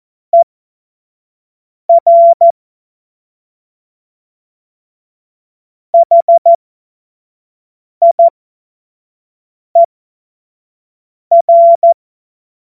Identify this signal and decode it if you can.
Morse code, heard as a keyed tone, about 14 wpm, spaced as for 5 wpm, 680 Hz. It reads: ER HIER
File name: Morsekurs / Witze / Witz 11. Juli 2023 25 F70